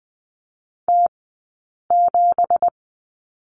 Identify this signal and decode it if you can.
Morse code, heard as a keyed tone, about 20 wpm, 700 Hz. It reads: T 7
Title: 204 Lesson 33d - Letter j - Two Characters 20wpm